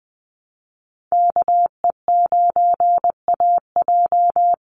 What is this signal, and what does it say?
Morse code, heard as a keyed tone, about 20 wpm, 700 Hz. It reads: KE9AJ
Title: NA SOTA Chaser Callsigns 20wpm